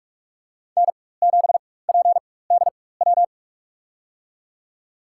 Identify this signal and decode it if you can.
Morse code, heard as a keyed tone, about 48 wpm, spaced as for 21 wpm, 700 Hz. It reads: N7PDW
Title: USA Call Signs - 5 Characters - 4x Character Spacing - 45wpm